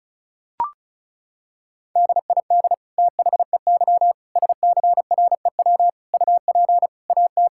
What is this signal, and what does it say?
Morse code, heard as a keyed tone, about 36 wpm, 700 Hz. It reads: DID THEY SCREW UP AT
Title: Sentences from Top 1600 Words 35wpm